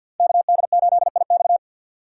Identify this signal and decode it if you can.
Morse code, heard as a keyed tone, about 53 wpm, 700 Hz. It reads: KD8IX
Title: USA Call Signs - 4x Character Spacing - 50wpm